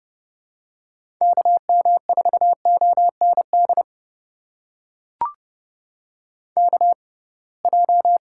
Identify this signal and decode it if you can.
Morse code, heard as a keyed tone, about 32 wpm, spaced as for 23 wpm, 700 Hz. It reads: KM4OND KJ
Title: USA Call Signs - 6 Characters - 6x Character Spacing - 30wpm